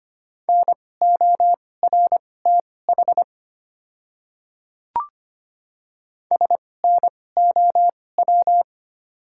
Morse code, heard as a keyed tone, about 25 wpm, spaced as for 18 wpm, 700 Hz. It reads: NORTH SNOW
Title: 2x Character Spacing - Top 500 Words - 25wpm